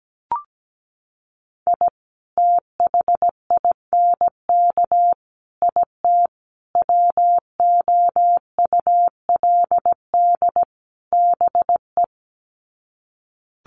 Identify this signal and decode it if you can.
Morse code, heard as a keyed tone, about 17 wpm, 700 Hz. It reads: I THINK IT WOULD BE
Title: Sets of 5 Words 17wpm